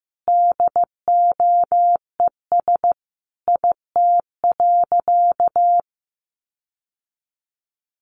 Morse code, heard as a keyed tone, about 15 wpm, 700 Hz. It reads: DOES IT.
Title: Sentences from Top 1500 Words 15wpm